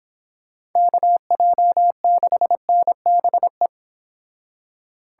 Morse code, heard as a keyed tone, about 26 wpm, 700 Hz. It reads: KJ6NBE
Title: Speed Racing - USA 2x3 Call Signs 22wpm